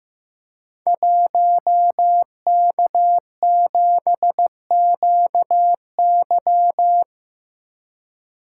Morse code, heard as a keyed tone, about 15 wpm, 700 Hz. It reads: 1K7QY
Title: Code Groups - 5 Letter:Numbers - 15wpm